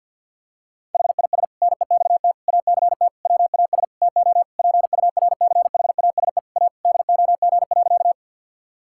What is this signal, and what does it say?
Morse code, heard as a keyed tone, about 52 wpm, 700 Hz. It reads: HIS NEXT ACT WAS TO PURCHASE A DOG.
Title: Sentences from Top 1000 Words - Review Part 5 - 50wpm